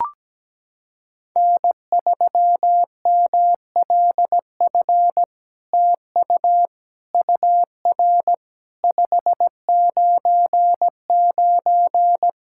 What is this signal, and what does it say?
Morse code, heard as a keyed tone, about 17 wpm, 700 Hz. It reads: N3MLF TU UR 599
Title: SOTA QSO Practice 17wpm